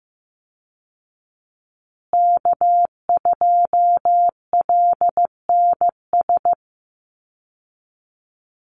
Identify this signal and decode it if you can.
Morse code, heard as a keyed tone, about 15 wpm, 700 Hz. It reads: K2LNS
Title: Call Signs Repeated 3x - 15wpm